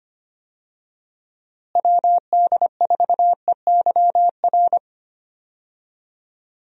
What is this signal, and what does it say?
Morse code, heard as a keyed tone, about 25 wpm, 700 Hz. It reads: WD4EYR
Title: USA Call Signs - 6 Characters - Repeated 3x - 25wpm